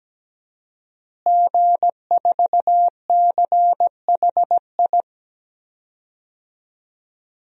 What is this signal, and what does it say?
Morse code, heard as a keyed tone, about 17 wpm, 700 Hz. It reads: G4CHI
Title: DXCC-100 Call Signs - 5 Characters - Repeated 2x - 17wpm